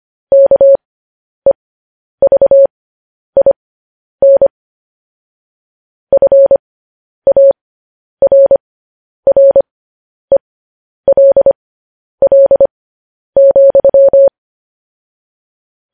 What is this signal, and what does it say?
Morse code, heard as a keyed tone, about 25 wpm, spaced as for 10 wpm, 550 Hz. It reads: KEVIN FARRELL,
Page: News Headlines in Morse Code at 10 WPM